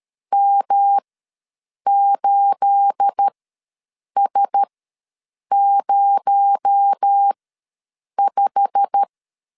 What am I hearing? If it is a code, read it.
Morse code, heard as a keyed tone, about 13 wpm, spaced as for 7 wpm, 800 Hz. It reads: M8S05
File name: LEZIONI / 9 28 ws 12wpm 3354